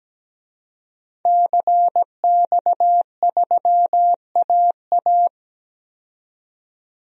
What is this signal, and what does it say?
Morse code, heard as a keyed tone, about 17 wpm, 700 Hz. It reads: CX3AA
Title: DXCC-100 Call Signs - 5 Characters - 17wpm